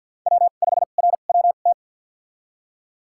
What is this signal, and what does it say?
Morse code, heard as a keyed tone, about 53 wpm, spaced as for 37 wpm, 700 Hz. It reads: W5RWT
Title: USA Call Signs - 5 Characters - 2x Character Spacing - 50wpm